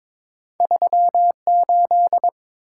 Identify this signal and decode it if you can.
Morse code, heard as a keyed tone, about 30 wpm, spaced as for 26 wpm, 700 Hz. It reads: 38
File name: Speed-Racing - Two Letter-Number Combinations - 22wpm